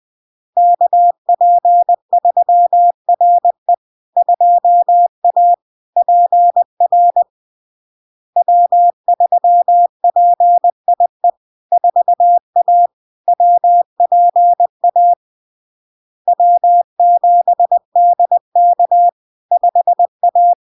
Morse code, heard as a keyed tone, about 20 wpm, 700 Hz. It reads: KP3RE 2A PR W3PIE 4A WPA W7DK 5A
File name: LARK-FD-2022-Log-in-CW-20wpm-700-Hz